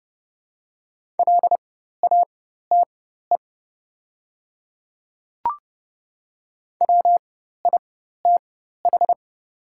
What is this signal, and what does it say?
Morse code, heard as a keyed tone, about 30 wpm, spaced as for 14 wpm, 700 Hz. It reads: LATE WITH
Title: 4x Character Spacing - Top 300 Words - 30wpm